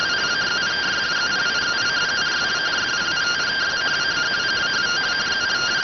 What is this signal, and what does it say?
Two-tone teleprinter signal, about 170 Hz shift.